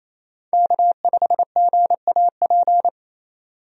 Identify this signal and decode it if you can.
Morse code, heard as a keyed tone, about 28 wpm, 700 Hz. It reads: K5GAP
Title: USA Call Signs - 5 Characters - 28wpm